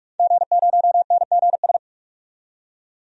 Morse code, heard as a keyed tone, about 46 wpm, 700 Hz. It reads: G0NGS
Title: DXCC-100 Call Signs - 5 Characters - 45wpm